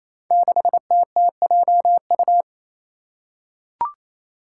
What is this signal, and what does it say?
Morse code, heard as a keyed tone, about 28 wpm, 700 Hz. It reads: 6TTJU
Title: Five Challenging Letter-Number Combinations 28wpm